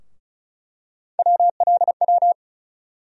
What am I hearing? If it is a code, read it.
Morse code, heard as a keyed tone, about 36 wpm, 700 Hz. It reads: WLW
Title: USA Call Sign Postfixes - 3 Letters 35wpm